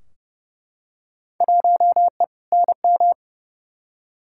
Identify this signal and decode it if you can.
Morse code, heard as a keyed tone, about 31 wpm, 700 Hz. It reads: 1E NM
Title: Field Day Exchanges - 30wpm